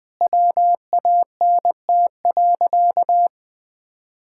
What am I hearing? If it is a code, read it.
Morse code, heard as a keyed tone, about 20 wpm, 700 Hz. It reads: WANT.
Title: Sentences with 5-Word Phrases - 20wpm